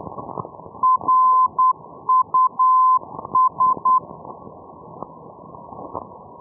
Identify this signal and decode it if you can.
Morse code, heard as a keyed tone, about 10 wpm, 1030 Hz. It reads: RUS
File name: europe / e-314 ndb reus rus-424-1305-day-menorca-rb